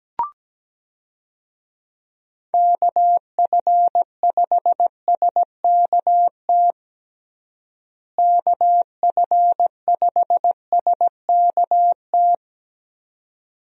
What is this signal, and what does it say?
Morse code, heard as a keyed tone, about 17 wpm, 700 Hz. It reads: KF5SKT KF5SKT
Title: USA Call Signs - 6 Characters - Repeated 2x - 17wpm